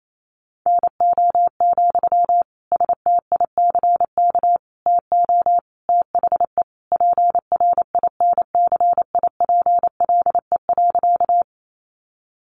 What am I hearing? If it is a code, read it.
Morse code, heard as a keyed tone, about 28 wpm, 700 Hz. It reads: NO, STICK TO THE PRINCIPLE.
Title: Sentences from Top 1400 Words 28wpm